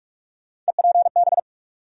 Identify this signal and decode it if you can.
Morse code, heard as a keyed tone, about 44 wpm, 700 Hz. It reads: EPB
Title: USA Call Sign Postfixes - 3 Letters 45wpm